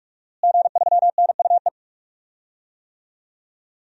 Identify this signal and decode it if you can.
Morse code, heard as a keyed tone, about 41 wpm, 700 Hz. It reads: G3NUE
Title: DXCC-100 Call Signs - 5 Characters - Repeated 3x - 45wpm